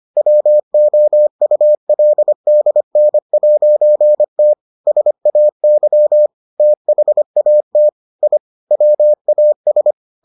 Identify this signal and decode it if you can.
Morse code, heard as a keyed tone, about 25 wpm, 600 Hz. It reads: WOULDN'T SAY THAT I WAS